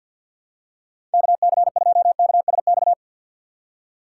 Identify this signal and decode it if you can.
Morse code, heard as a keyed tone, about 48 wpm, 700 Hz. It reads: KC2KSX
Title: USA Call Signs - 3x Character Spacing - 50wpm